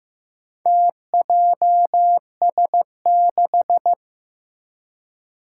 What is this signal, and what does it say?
Morse code, heard as a keyed tone, about 15 wpm, 700 Hz. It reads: TJS6
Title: Four Challenging Letter-Number Combinations 15wpm